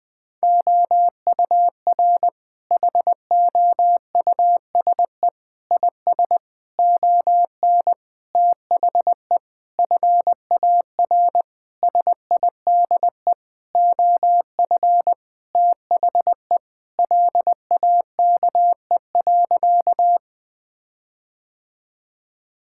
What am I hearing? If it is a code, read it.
Morse code, heard as a keyed tone, about 19 wpm, 700 Hz. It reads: OUR HOUSE IS ON THE FAR SIDE OF THE LAKE.
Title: Sentences with 5-Word Phrases - 20wpm